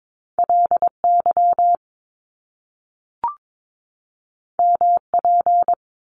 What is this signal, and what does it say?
Morse code, heard as a keyed tone, about 22 wpm, 700 Hz. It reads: LY MP
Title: Top Two Letter Combinations in Words 22wpm